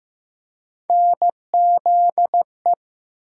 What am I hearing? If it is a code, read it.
Morse code, heard as a keyed tone, about 15 wpm, 700 Hz. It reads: NZE
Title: USA Call Sign Postfixes - 3 Letters 15wpm